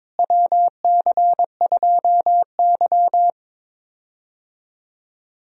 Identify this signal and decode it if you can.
Morse code, heard as a keyed tone, about 22 wpm, 700 Hz. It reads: WC2Y
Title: USA Call Signs - 4 Characters - Repeated 2x - 22wpm